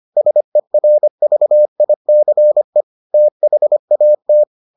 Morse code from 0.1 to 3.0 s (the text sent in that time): SERVICE